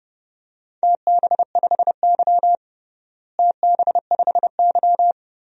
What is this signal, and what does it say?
Morse code, heard as a keyed tone, about 32 wpm, 700 Hz. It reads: TB5Y TB5Y
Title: DXCC-100 Call Signs - 4 Characters - Repeated 3x - 30wpm